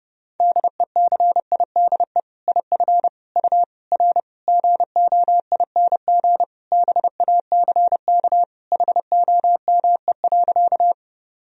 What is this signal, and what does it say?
Morse code, heard as a keyed tone, about 30 wpm, 700 Hz. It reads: DECIDE IF U R GOING BACK HOME.
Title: Sentences from Top 500 Words and Abbreviations 30wpm